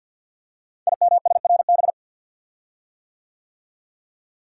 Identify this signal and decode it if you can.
Morse code, heard as a keyed tone, about 49 wpm, 700 Hz. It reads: IMSRB
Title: Code Groups - 5 Letters - 50wpm